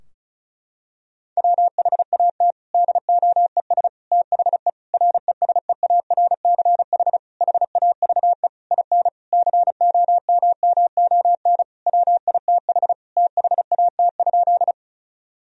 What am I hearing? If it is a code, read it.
Morse code, heard as a keyed tone, about 34 wpm, 700 Hz. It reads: WHAT DOES THE RESEARCH HAVE IN COMMON WITH THAT?
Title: Sentences from Top 1000 Words - Review Part 4 - 35wpm